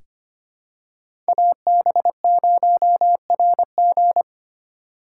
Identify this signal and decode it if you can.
Morse code, heard as a keyed tone, about 25 wpm, 700 Hz. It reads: AB0RG
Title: USA Call Signs - 5 Characters - 4x Character Spacing - 25wpm